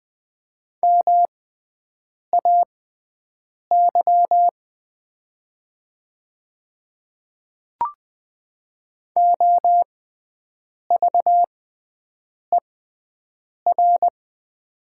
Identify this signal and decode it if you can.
Morse code, heard as a keyed tone, about 20 wpm, spaced as for 7 wpm, 700 Hz. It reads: MAY OVER
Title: 6x Character Spacing - Top 100 Words - 20wpm